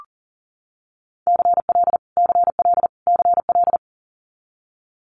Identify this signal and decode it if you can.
Morse code, heard as a keyed tone, about 38 wpm, 700 Hz. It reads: CL CL CL
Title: Multi-letter Phonemes Repeated 3x 40wpm